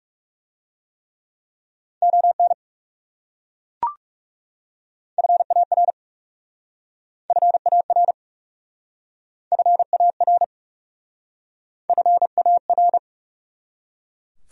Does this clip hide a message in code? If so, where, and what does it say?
Yes: Morse code from 2.0 to 13.0 s, heard as a keyed tone, about 38 wpm, 700 Hz. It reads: ON FAR FAR FAR FAR